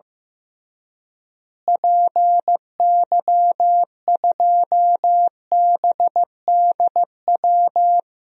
Morse code, heard as a keyed tone, about 15 wpm, 700 Hz. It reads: PY2BDW
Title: DXCC-100 Call Signs - 6 Characters - Repeated 3x - 15wpm